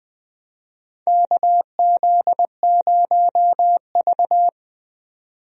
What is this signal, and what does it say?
Morse code, heard as a keyed tone, about 21 wpm, 700 Hz. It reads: KZ0V
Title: USA Call Signs - 4 Characters - 20wpm